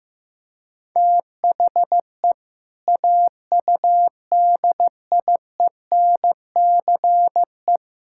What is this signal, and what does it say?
Morse code, heard as a keyed tone, about 14 wpm, 700 Hz. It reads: THE AUDIENCE
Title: Sentences from Top 1000 Words - Review Part 5 - 15wpm